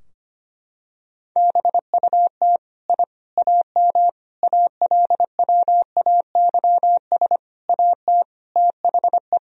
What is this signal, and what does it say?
Morse code, heard as a keyed tone, about 25 wpm, 700 Hz. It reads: BUT I AM ALWAYS AT THE
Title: Sentences from Top 1000 Words - Review Part 2 - 25wpm